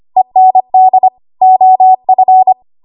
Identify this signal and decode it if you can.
Morse code, heard as a keyed tone, about 25 wpm, 750 Hz. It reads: END OF
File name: Archive / 25 WPM / 25WPM